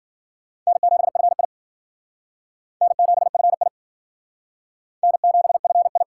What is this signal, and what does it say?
Morse code, heard as a keyed tone, about 52 wpm, 690 Hz. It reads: N7FI N7FI N7FI